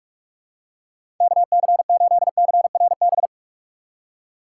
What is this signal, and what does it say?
Morse code, heard as a keyed tone, about 43 wpm, 700 Hz. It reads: KC8CRB
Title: USA Call Signs - 6 Characters - 6x Character Spacing - 45wpm